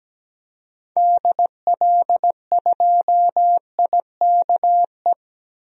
Morse code, heard as a keyed tone, about 17 wpm, 700 Hz. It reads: DL2IKE
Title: DXCC-100 Call Signs - 6 Characters - 17wpm